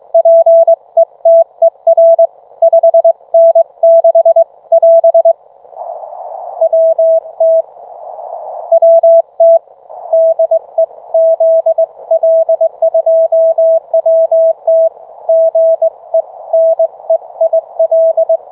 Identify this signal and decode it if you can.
Morse code, heard as a keyed tone, about 22 wpm, 660 Hz. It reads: PETER 5N6 <AS> WT WT DE ZL2WT GE NEIL